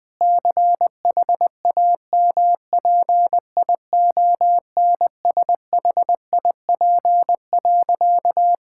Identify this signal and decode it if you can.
Morse code, heard as a keyed tone, about 21 wpm, 700 Hz. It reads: CHAMPIONSHIP.
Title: Sentences with 8-Word Phrases - Spoken Beginning - 20wpm